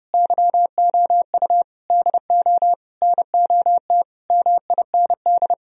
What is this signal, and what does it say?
Morse code, heard as a keyed tone, about 37 wpm, 700 Hz. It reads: YOU DO NOT MIND
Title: Speed-Racing - Sets of 5 Words 30wpm